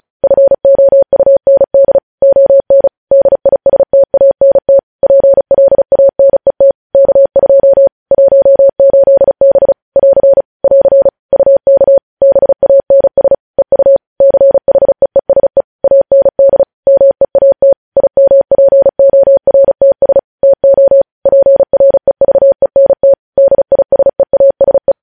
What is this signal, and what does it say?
Morse code, heard as a keyed tone, about 36 wpm, 550 Hz. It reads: FOUND ON DISTANT PLANET K2 18B <AR> <AR> UK BANS EU CHEESE AND MEAT IMPORTS TO PREVENT DISEASE
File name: podcast / News headlines-1x-35x35